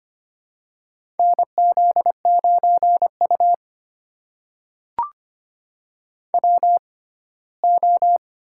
Morse code, heard as a keyed tone, about 25 wpm, spaced as for 17 wpm, 700 Hz. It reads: NZ9U WO
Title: USA Call Signs - 4 Characters - 6x Character Spacing - 25wpm